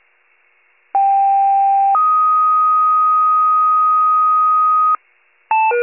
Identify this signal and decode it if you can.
Two-tone page, heard as a one-second tone, then a three-second tone, about 790 then 1250 Hz. It is heard from 0.9 to 5.0 s.